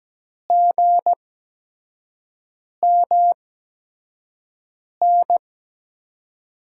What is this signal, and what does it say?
Morse code, heard as a keyed tone, about 17 wpm, spaced as for 5 wpm, 700 Hz. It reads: GMN
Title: USA Call Signs - 5 Characters - 8x Character Spacing - 17wpm